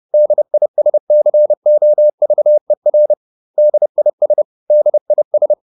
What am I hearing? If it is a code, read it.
Morse code, heard as a keyed tone, about 30 wpm, 600 Hz. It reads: DISCOVER DIS DIS